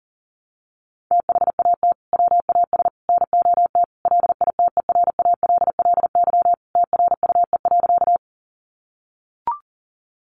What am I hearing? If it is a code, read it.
Morse code, heard as a keyed tone, about 40 wpm, 700 Hz. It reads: THAT WAS NOT LITERALLY TRUE.
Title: Sentences from Top 1400 Words 40wpm